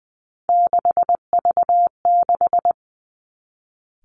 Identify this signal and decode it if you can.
Morse code, heard as a keyed tone, about 20 wpm, 700 Hz. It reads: 6V6
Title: Three Challenging Letter-Number Combinations 20wpm